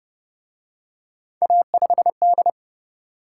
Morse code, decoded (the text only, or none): A5D